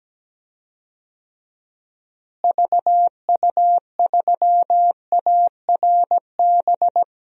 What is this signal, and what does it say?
Morse code, heard as a keyed tone, about 17 wpm, 700 Hz. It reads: VU3ARB
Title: DXCC-100 Call Signs - 6 Characters - Repeated 2x - 17wpm